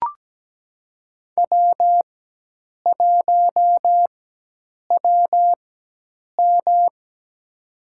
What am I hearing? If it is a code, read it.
Morse code, heard as a keyed tone, about 17 wpm, spaced as for 8 wpm, 700 Hz. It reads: W1WM
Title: USA Call Signs - 4 Characters - 4x Character Spacing - 17wpm